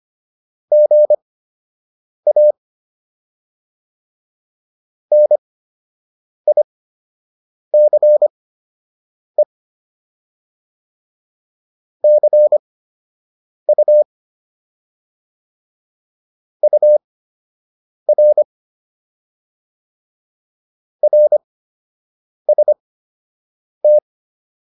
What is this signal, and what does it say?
Morse code, heard as a keyed tone, about 25 wpm, spaced as for 7 wpm, 600 Hz. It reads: GA NICE CU UR RST